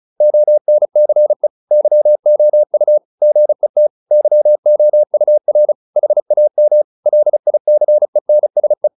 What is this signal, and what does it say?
Morse code, heard as a keyed tone, about 35 wpm, 600 Hz. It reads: ONCE YOU GET YOUR HAM LICENSE